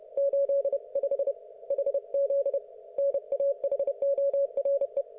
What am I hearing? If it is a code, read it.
Morse code, heard as a keyed tone, about 31 wpm, 560 Hz. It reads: 85 HZ NAHORE
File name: Archiv / CUC179 / ok1fhi cul179